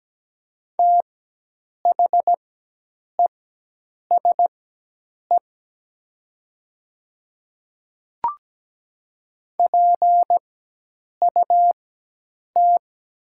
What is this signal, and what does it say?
Morse code, heard as a keyed tone, about 17 wpm, spaced as for 8 wpm, 700 Hz. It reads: THESE PUT
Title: 4x Character Spacing - Top 200 Words - 17wpm